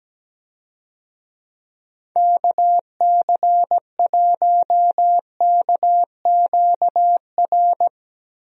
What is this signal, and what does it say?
Morse code, heard as a keyed tone, about 17 wpm, 700 Hz. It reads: KC1KQR
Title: USA Call Signs - 6 Characters - Repeated 2x - 17wpm